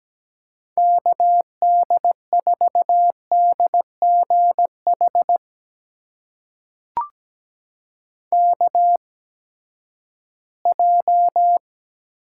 Morse code, heard as a keyed tone, about 17 wpm, spaced as for 12 wpm, 700 Hz. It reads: KD4DGH KJ
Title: USA Call Signs - 6 Characters - 8x Character Spacing - 17wpm